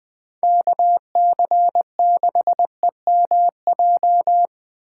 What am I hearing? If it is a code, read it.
Morse code, heard as a keyed tone, about 24 wpm, 700 Hz. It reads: KC6EMJ
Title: Speed Racing - USA 2x3 Call Signs 20wpm